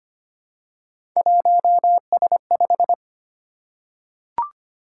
Morse code, heard as a keyed tone, about 25 wpm, 700 Hz. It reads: 1S5
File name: Three Letter-Number Combinations - 25wpm